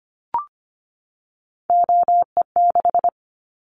Morse code, heard as a keyed tone, about 25 wpm, 700 Hz. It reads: OE6
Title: Callsign Prefixes for Top 50 DXCC - 25wpm